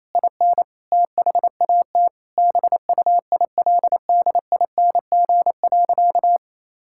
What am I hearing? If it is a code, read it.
Morse code, heard as a keyed tone, about 28 wpm, 700 Hz. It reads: IN THAT BUILDING.
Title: Sentences from Top 1000 Words - Review Part 5 - 28wpm